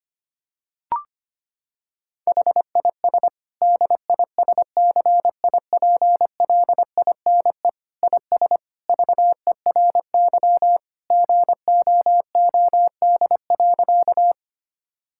Morse code, heard as a keyed tone, about 25 wpm, 700 Hz. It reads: HIS DISCIPLINE IS VERY GOOD.
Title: Sentences from Top 1000 Words - Review Part 6 - 25wpm